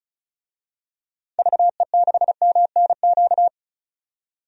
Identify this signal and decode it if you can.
Morse code, heard as a keyed tone, about 34 wpm, 700 Hz. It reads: VE6MNQ
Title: DXCC-100 Call Signs - 6 Characters - 35wpm